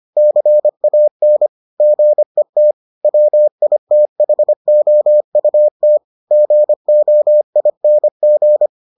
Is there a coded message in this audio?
Yes: Morse code from 0.2 to 8.7 s, heard as a keyed tone, about 25 wpm, 600 Hz. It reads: CAN GET WITHOUT GOING